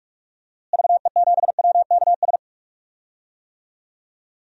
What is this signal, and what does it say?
Morse code, heard as a keyed tone, about 45 wpm, 700 Hz. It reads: VE7WKS
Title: DXCC-100 Call Signs - 6 Characters - Repeated 3x - 45wpm